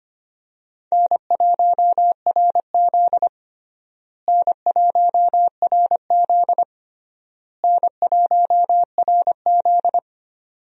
Morse code, heard as a keyed tone, about 25 wpm, 700 Hz. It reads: N1RZ N1RZ N1RZ
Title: USA Call Signs - 4 Characters - Repeated 3x - 25wpm